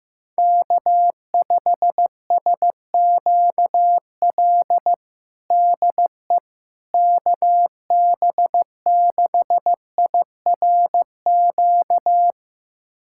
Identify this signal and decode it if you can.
Morse code, heard as a keyed tone, about 15 wpm, 700 Hz. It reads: K5SQL DE KB6IRQ
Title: USA Callsign DE Callsign 15wpm